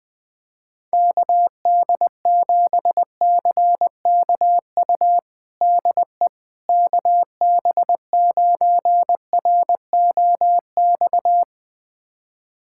Morse code, heard as a keyed tone, about 21 wpm, 700 Hz. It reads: KD7CKU DE KB9ROX
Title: USA Callsign DE Callsign 20wpm